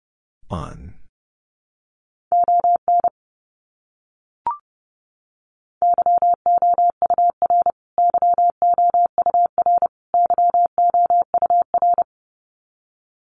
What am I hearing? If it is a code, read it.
Morse code, heard as a keyed tone, about 30 wpm, 700 Hz. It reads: ON YOUR YOUR YOUR
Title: Top 200 Words Repeated 30wpm